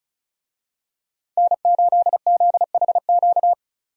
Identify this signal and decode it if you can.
Morse code, heard as a keyed tone, about 36 wpm, 700 Hz. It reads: N8ZHQ
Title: USA Call Signs - 5 Characters - 35wpm